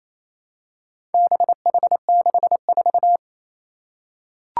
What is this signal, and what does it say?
Morse code, heard as a keyed tone, about 29 wpm, 700 Hz. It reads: BH64
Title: Four Challenging Letter-Number Combinations 28wpm